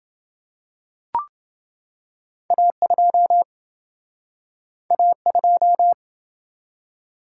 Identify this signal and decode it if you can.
Morse code, heard as a keyed tone, about 28 wpm, 700 Hz. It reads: A2 A2